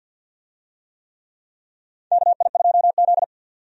Morse code, heard as a keyed tone, about 52 wpm, 700 Hz. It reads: KI2Z
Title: USA Call Signs - 4 Characters - 50wpm